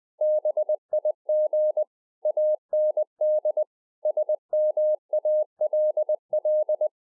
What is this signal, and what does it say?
Morse code, heard as a keyed tone, about 20 wpm, 610 Hz. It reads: BIG AND SMALL